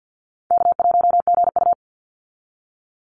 Morse code, heard as a keyed tone, about 48 wpm, 700 Hz. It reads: K1GU
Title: Easy Call Signs Repeated 2x - 50wpm